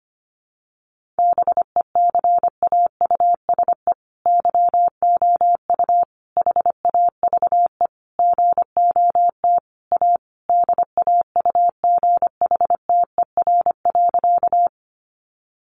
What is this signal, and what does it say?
Morse code, heard as a keyed tone, about 25 wpm, 700 Hz. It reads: BECAUSE YOU HAVE GOT A DAUGHTER.
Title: Sentences from Top 1500 Words 25wpm